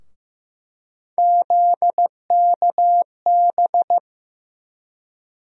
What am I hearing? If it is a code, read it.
Morse code, heard as a keyed tone, about 15 wpm, 700 Hz. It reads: ZKB
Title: USA Call Sign Postfixes - 3 Letters 15wpm